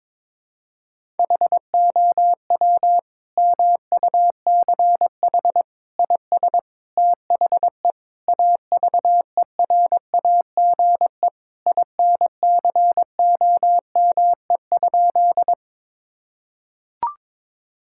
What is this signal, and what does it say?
Morse code, heard as a keyed tone, about 22 wpm, 700 Hz. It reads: HOW MUCH IS THE AVERAGE INCOME?
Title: Sentences from Top 1000 Words - Review Part 6 - 22wpm